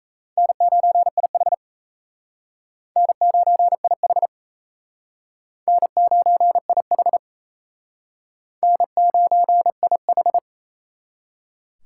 Morse code, heard as a keyed tone, about 35 wpm, 700 Hz. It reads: N9IH N9IH N9IH N9IH
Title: Speed Racing - USA 1x2 Call Signs 28wpm